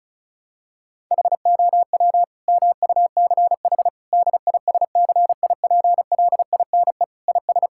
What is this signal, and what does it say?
Morse code, heard as a keyed tone, about 36 wpm, 700 Hz. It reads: HOW MUCH DISCIPLINE IS
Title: Sentences from Top 1000 Words - Review Part 6 - 35wpm